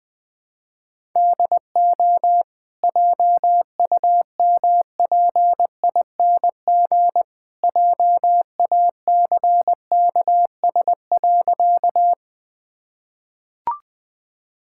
Morse code, heard as a keyed tone, about 20 wpm, 700 Hz. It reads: DO JUMPING JACKS.
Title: Easy Sentences with Hints - 20wpm